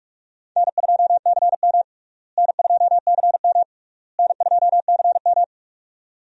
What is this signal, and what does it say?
Morse code, heard as a keyed tone, about 45 wpm, 700 Hz. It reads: N2CM N2CM N2CM
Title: USA Call Signs - 4 Characters - Repeated 3x - 45wpm